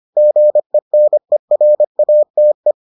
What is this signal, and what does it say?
Morse code, heard as a keyed tone, about 25 wpm, 600 Hz. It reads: GENERATE